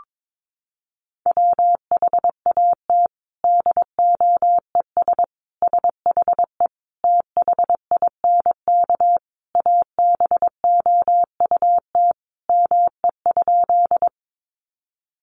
Morse code, heard as a keyed tone, about 22 wpm, 700 Hz. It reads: WHAT DOES SHE THINK ABOUT ME?
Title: Sentences from Top 200 Words 22wpm